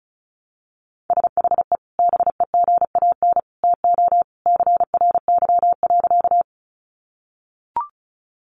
Morse code, heard as a keyed tone, about 34 wpm, 700 Hz. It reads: SHE BEGAN TO CRY.
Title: Sentences from Top 500 Words 35wpm